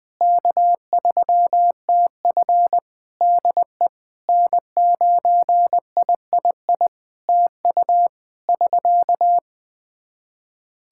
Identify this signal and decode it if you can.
Morse code, heard as a keyed tone, about 20 wpm, 700 Hz. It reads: K3TF DE N9III TU <SK>
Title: Practice QSO - Exchange 4 - Ending - 20wpm